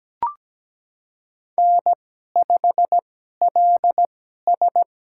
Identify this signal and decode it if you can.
Morse code, heard as a keyed tone, about 17 wpm, spaced as for 12 wpm, 700 Hz. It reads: N5LS
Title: USA Call Signs - 2x Character Spacing - 17wpm